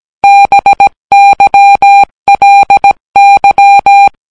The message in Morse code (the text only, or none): BYLY